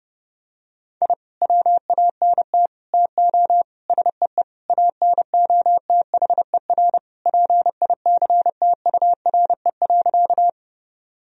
Morse code, heard as a keyed tone, about 30 wpm, 700 Hz. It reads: I WANT TO SEE ANOTHER PICTURE.
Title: Sentences from Top 1000 Words - Review Part 1 - 30wpm